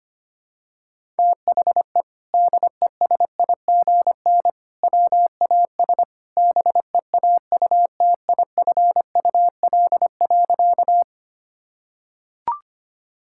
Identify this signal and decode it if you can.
Morse code, heard as a keyed tone, about 25 wpm, 700 Hz. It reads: THE DESIGN WAS BEAUTIFUL.
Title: Sentences from Top 1000 Words - Review Part 5 - 25wpm